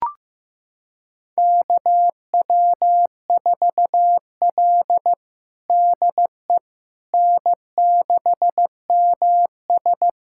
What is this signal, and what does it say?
Morse code, heard as a keyed tone, about 16 wpm, 700 Hz. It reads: KW4L DE N6MS
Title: USA Callsign DE Callsign - 4 Characters - 15wpm